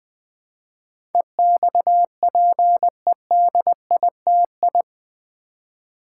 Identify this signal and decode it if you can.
Morse code, heard as a keyed tone, about 20 wpm, 700 Hz. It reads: EXPEDITI
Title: Build A Word - 4 Letters At A Time - 10-Letter Words - 20wpm